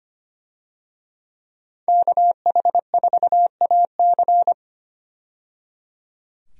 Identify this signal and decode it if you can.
Morse code, heard as a keyed tone, about 25 wpm, 690 Hz. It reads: KH4AC
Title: USA Call Signs - 5 Characters - 25wpm